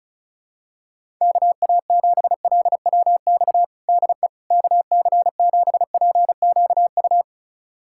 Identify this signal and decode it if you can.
Morse code, heard as a keyed tone, about 34 wpm, 700 Hz. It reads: KA7LWX DE KC7PQU
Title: USA Callsign DE Callsign - 6 Characters - 35wpm